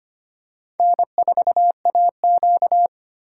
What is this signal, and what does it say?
Morse code, heard as a keyed tone, about 25 wpm, 700 Hz. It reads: N4AQ
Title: USA Call Signs - 2x Character Spacing - 25wpm